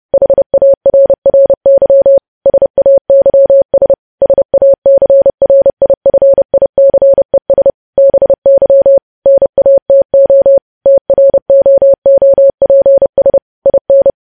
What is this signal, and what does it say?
Morse code, heard as a keyed tone, about 30 wpm, 550 Hz. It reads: HARRY SAYS SACRIFICES BY NATO TROOPS IN